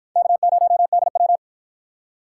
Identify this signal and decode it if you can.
Morse code, heard as a keyed tone, about 50 wpm, 700 Hz. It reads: K0DW